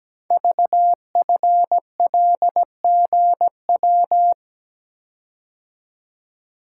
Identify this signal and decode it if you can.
Morse code, heard as a keyed tone, about 17 wpm, 700 Hz. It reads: VFLGW
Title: Code Groups - 5 Letters - 17wpm